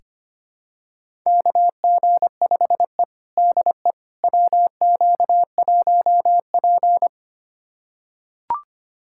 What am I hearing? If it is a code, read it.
Morse code, heard as a keyed tone, about 25 wpm, 700 Hz. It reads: KG5E DE WQ1P
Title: USA Callsign DE Callsign - 4 Characters - 25wpm